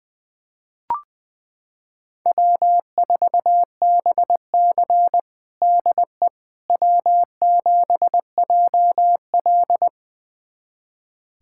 Morse code, heard as a keyed tone, about 20 wpm, 700 Hz. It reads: W4BC DE W7JL
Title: USA Callsign DE Callsign - 4 Characters - 20wpm